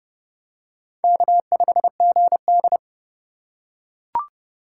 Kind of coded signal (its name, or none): Morse code